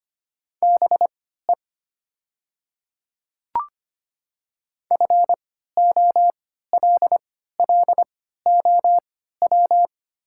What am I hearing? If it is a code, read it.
Morse code, heard as a keyed tone, about 25 wpm, spaced as for 14 wpm, 700 Hz. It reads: BE FOLLOW
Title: 3x Character Spacing - Top 200 Words - 25wpm